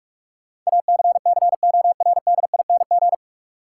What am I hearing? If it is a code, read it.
Morse code, heard as a keyed tone, about 46 wpm, 700 Hz. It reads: ACCORDING
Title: Sentences from Top 700 Words - New Word Drill - 45wpm